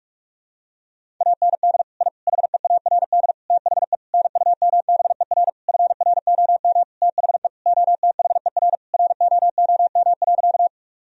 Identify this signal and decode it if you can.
Morse code, heard as a keyed tone, about 46 wpm, 700 Hz. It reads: AND I HEARD THE NUMBER FROM THE OTHER ROOM.
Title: Sentences from Top 1000 Words - Review Part 1 - 45wpm